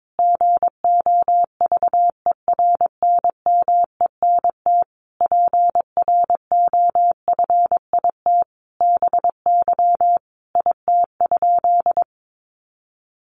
Morse code, heard as a keyed tone, about 22 wpm, 700 Hz. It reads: GOVERNMENT PROFIT BY IT?
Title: Sentences from Top 800 Words 22wpm